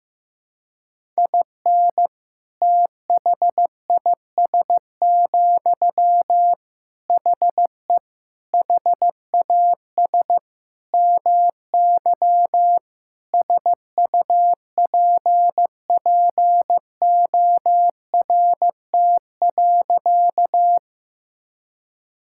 Morse code, heard as a keyed tone, about 16 wpm, 700 Hz. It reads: IN THIS, HE HAS MY SUPPORT.
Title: Sentences from Top 1000 Words - Review Part 5 - 15wpm